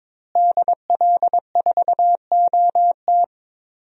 Morse code, heard as a keyed tone, about 22 wpm, 700 Hz. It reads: DL4OT
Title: USA Call Signs - 5 Characters - Repeated 3x - 22wpm